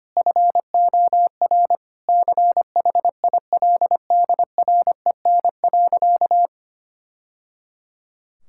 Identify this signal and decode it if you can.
Morse code, heard as a keyed tone, about 25 wpm, 700 Hz. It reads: FOR CHILDREN.
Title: Sentences from Top 1000 Words - Review Part 5 - 25wpm